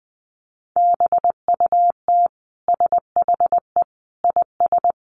Morse code, heard as a keyed tone, about 20 wpm, 700 Hz. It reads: BUT SHE IS
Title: Sentences from Top 1500 Words 20wpm